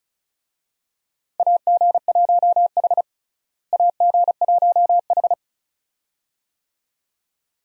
Morse code, heard as a keyed tone, about 36 wpm, 700 Hz. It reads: AG1H AG1H
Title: USA Call Signs - 4 Characters - Repeated 2x - 35wpm